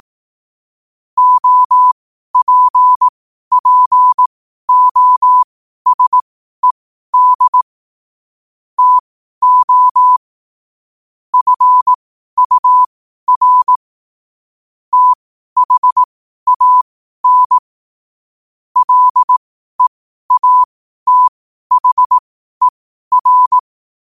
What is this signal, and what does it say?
Morse code, heard as a keyed tone, about 18 wpm, spaced as for 12 wpm, 1000 Hz. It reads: OPPOSED TO FUR THAN LEATHER
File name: qotdfiles / 12wpm